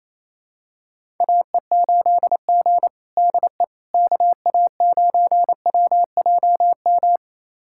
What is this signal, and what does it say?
Morse code, heard as a keyed tone, about 28 wpm, 700 Hz. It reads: AE8G DE KA9WJM
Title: USA Callsign DE Callsign 28wpm